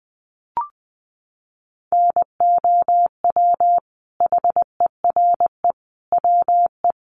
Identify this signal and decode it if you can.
Morse code, heard as a keyed tone, about 20 wpm, 700 Hz. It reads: NOW HERE WE
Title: Sentences from Top 1200 Words 20wpm